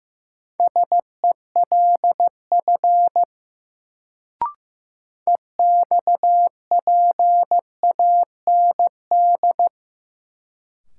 Morse code, heard as a keyed tone, about 15 wpm, 700 Hz. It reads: SELF EXPAND
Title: Sentences from Top 1400 Words - New Word Drill Alt - 15wpm